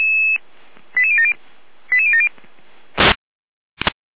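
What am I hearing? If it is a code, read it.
Five-tone selective call (ZVEI), heard as quick runs of five tones, tones of about 70 ms. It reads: 88070 88070
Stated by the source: ZVEI five-tone selective call 89313, received via SDR.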